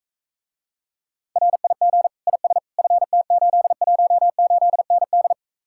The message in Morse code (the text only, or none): RIG IS FT818ND